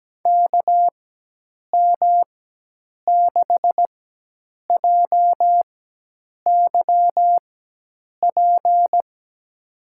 Morse code, heard as a keyed tone, about 17 wpm, spaced as for 8 wpm, 700 Hz. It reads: KM6JYP
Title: USA Call Signs - 4x Character Spacing - 17wpm